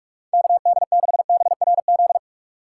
Morse code, heard as a keyed tone, about 47 wpm, 700 Hz. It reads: KD6BRZ